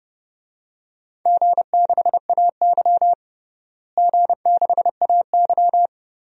Morse code, heard as a keyed tone, about 30 wpm, 700 Hz. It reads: G6AY G6AY
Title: DXCC-100 Call Signs - 4 Characters - Repeated 3x - 30wpm